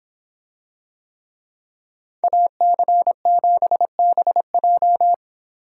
Morse code, heard as a keyed tone, about 25 wpm, 700 Hz. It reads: AC7BJ